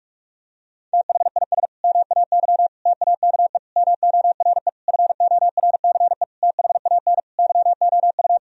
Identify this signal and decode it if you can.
Morse code, heard as a keyed tone, about 46 wpm, 700 Hz. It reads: THIS MAY TAKE MORE FORCE THAN YOU
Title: Sentences from Top 1000 Words - Review Part 3 - 45wpm